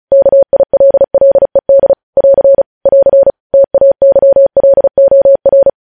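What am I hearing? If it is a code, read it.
Morse code, heard as a keyed tone, about 36 wpm, 550 Hz. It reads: KILLED <AR> <AR> TAYLOR
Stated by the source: Morse code practice reading news headlines at 35 words per minute.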